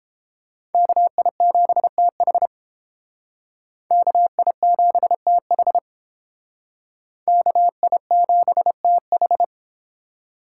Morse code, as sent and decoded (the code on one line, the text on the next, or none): -.- .. --... - .... / -.- .. --... - .... / -.- .. --... - ....
KI7TH KI7TH KI7TH